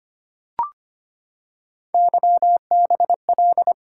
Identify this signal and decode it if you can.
Morse code, heard as a keyed tone, about 25 wpm, 700 Hz. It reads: YBL
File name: Three Letter-Number Combinations - 25wpm